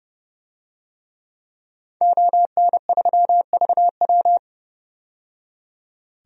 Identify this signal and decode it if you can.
Morse code, heard as a keyed tone, about 31 wpm, 700 Hz. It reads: ON3VW
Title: DXCC-100 Call Signs - 5 Characters - 30wpm